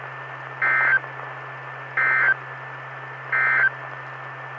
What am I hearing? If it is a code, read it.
Emergency Alert System (SAME) tones, heard as harsh buzzing bursts. It is heard from 0.5 to 3.8 s.
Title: Saline, AK SKYWARN Repeater System - W5RHS